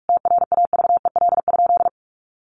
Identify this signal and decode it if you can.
Morse code, heard as a keyed tone, about 44 wpm, 700 Hz. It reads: TRAVEL?
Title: Sentences from Top 1400 Words 45wpm